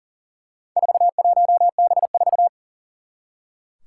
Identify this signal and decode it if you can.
Morse code, heard as a keyed tone, about 42 wpm, 700 Hz. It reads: 41B4
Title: Four Challenging Letter-Number Combinations 40wpm